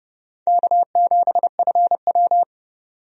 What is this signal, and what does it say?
Morse code, heard as a keyed tone, about 29 wpm, 700 Hz. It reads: K7FW
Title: USA Call Signs - 4 Characters - 30wpm